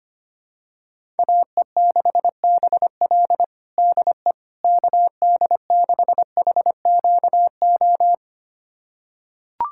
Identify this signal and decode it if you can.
Morse code, heard as a keyed tone, about 25 wpm, 700 Hz. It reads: AE6BL DE KD6HQO
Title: USA Callsign DE Callsign 25wpm